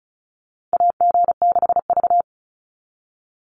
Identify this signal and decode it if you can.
Morse code, heard as a keyed tone, about 34 wpm, 700 Hz. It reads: AG6V
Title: NA SOTA Chaser Callsigns 35wpm